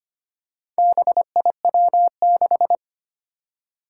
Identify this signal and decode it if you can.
Morse code, heard as a keyed tone, about 25 wpm, 700 Hz. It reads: BIW6
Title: Four Challenging Letter-Number Combinations 25wpm